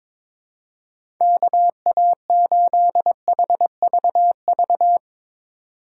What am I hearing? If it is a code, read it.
Morse code, heard as a keyed tone, about 21 wpm, 700 Hz. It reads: KA8HVV
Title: USA Call Signs - 6 Characters - 2x Character Spacing - 22wpm